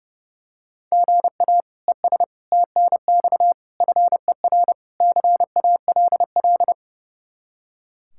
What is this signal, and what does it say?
Morse code, heard as a keyed tone, about 30 wpm, 700 Hz. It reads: GA ES TNX FER CALL
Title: Practice QSO - Exchange 1 - Beginning - 30wpm